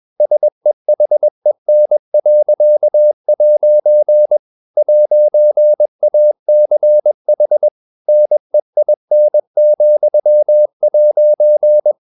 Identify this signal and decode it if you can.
Morse code, heard as a keyed tone, about 21 wpm, 600 Hz. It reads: SEHEN.' 'ACH NEIN,'